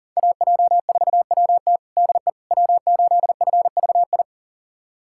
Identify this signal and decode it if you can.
Morse code, heard as a keyed tone, about 37 wpm, 700 Hz. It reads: AJ4WT DE W8FVI
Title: USA Callsign DE Callsign - 5 Characters - 40wpm